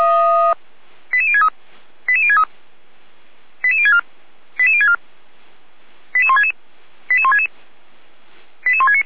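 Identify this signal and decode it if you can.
Five-tone selective call (ZVEI), heard as quick runs of five tones, tones of about 70 ms. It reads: 88063 88063 88064 88064 88166 88166 88169
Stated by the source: SDR-received ZVEI five-tone call 88352.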